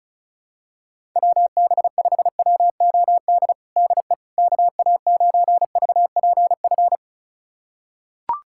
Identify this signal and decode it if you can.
Morse code, heard as a keyed tone, about 36 wpm, 700 Hz. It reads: WB5WOD DE KA9VPF
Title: USA Callsign DE Callsign - 6 Characters - 35wpm